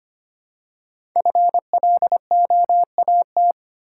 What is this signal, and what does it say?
Morse code, heard as a keyed tone, about 25 wpm, 700 Hz. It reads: FLOAT
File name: 5 Letter Words - First 3 Letters Given - 25wpm